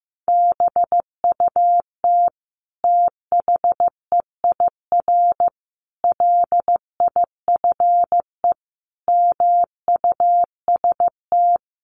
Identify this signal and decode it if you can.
Morse code, heard as a keyed tone, about 15 wpm, 700 Hz. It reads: BUT THEIR LIFE MUST
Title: Sentences from Top 1400 Words 15wpm